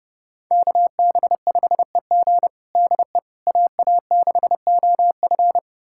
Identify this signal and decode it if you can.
Morse code, heard as a keyed tone, about 32 wpm, 700 Hz. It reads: KB5EG DE AA6OF
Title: USA Callsign DE Callsign - 5 Characters - 30wpm